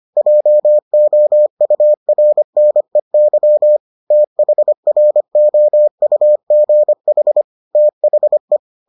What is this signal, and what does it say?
Morse code, heard as a keyed tone, about 25 wpm, 600 Hz. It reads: JOURNEY THROUGH THE